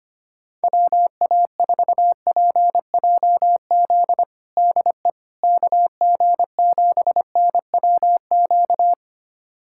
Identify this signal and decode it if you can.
Morse code, heard as a keyed tone, about 25 wpm, 700 Hz. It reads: WA4PJZ DE KG7NWQ
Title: USA Callsign DE Callsign 25wpm